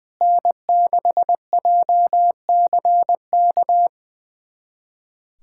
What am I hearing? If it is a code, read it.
Morse code, heard as a keyed tone, about 20 wpm, 700 Hz. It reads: N6JCK
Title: USA Callsign DE Callsign - 5 Characters - 20wpm